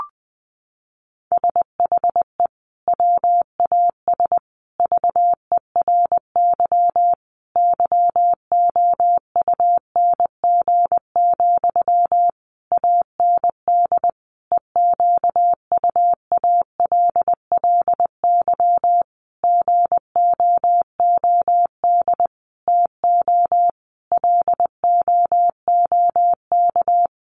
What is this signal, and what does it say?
Morse code, heard as a keyed tone, about 20 wpm, 700 Hz. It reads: SHE WAS VERY YOUNG, AND EQUALLY GOOD TO LOOK
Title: Sentences from Top 1400 Words 20wpm